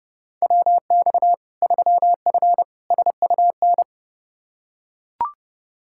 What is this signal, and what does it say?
Morse code, heard as a keyed tone, about 32 wpm, 700 Hz. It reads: WX 3F SUN
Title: Practice QSO - Exchange 2 - Info - 30wpm